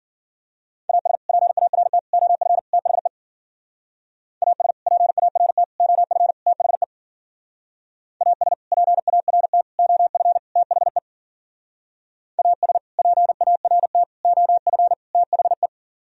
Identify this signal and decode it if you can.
Morse code, heard as a keyed tone, about 48 wpm, 700 Hz. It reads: AS PART OF THE AS PART OF THE AS PART OF THE AS PART OF THE
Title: Speed-Racing - Sets of 4 Words 40wpm